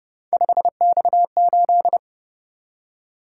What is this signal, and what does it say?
Morse code, heard as a keyed tone, about 30 wpm, 700 Hz. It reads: 5X8
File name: Speed-Racing - Three Letter-Number Combinations - 25wpm